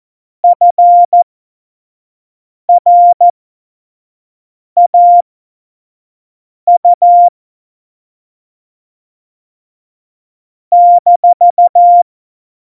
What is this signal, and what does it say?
Morse code, heard as a keyed tone, about 14 wpm, spaced as for 5 wpm, 700 Hz. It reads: FRAU -